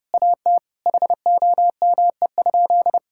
Morse code, heard as a keyed tone, about 30 wpm, 700 Hz. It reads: AT HOME?
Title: Sentences from Top 1000 Words - Review Part 6 - 30wpm